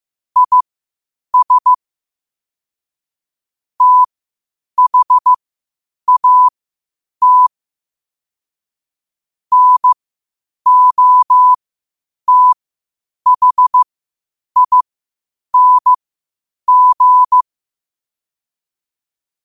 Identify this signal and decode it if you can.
Morse code, heard as a keyed tone, about 15 wpm, spaced as for 8 wpm, 1000 Hz. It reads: IS THAT NOTHING
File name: qotdfiles / 8wpm